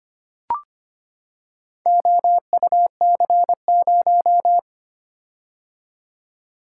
Morse code, heard as a keyed tone, about 25 wpm, 700 Hz. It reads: OUC0
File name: practice / 25wpm / Four Letter-Number Combinations - 25wpm